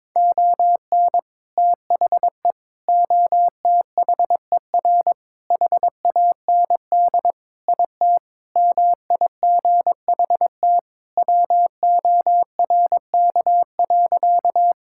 Morse code, heard as a keyed tone, about 22 wpm, 700 Hz. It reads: ON THE OTHER HAND IT MIGHT WORK.
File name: practice / 22wpm / Sentences with 3-Word Phrases - 22wpm